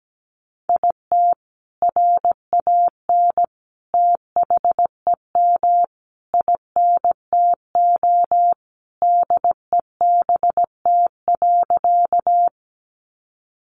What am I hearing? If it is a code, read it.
Morse code, heard as a keyed tone, about 17 wpm, 700 Hz. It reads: IT RAN THEM INTO DEBT.
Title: Sentences from Top 1300 Words 17wpm